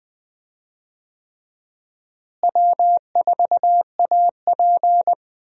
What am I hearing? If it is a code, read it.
Morse code, heard as a keyed tone, about 20 wpm, 700 Hz. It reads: W4AP
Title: USA Call Signs - 4 Characters - Repeated 3x - 20wpm